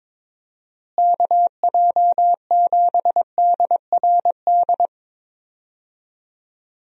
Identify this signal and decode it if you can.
Morse code, heard as a keyed tone, about 22 wpm, 700 Hz. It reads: KJ7DRD
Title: USA Call Signs - 6 Characters - Repeated 2x - 22wpm